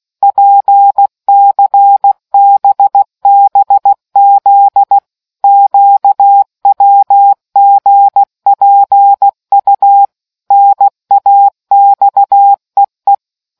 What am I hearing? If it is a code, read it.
Morse code, heard as a keyed tone, about 16 wpm, 800 Hz. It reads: PCBBZ QWGPU NAXEE